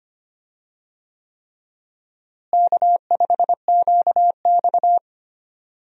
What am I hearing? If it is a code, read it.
Morse code, heard as a keyed tone, about 25 wpm, 700 Hz. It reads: K5QX
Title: USA Call Signs - 4 Characters - Repeated 3x - 25wpm